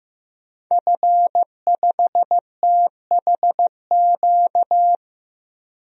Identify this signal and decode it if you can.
Morse code, heard as a keyed tone, about 15 wpm, 700 Hz. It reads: F5THQ
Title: DXCC-100 Call Signs - 5 Characters - 15wpm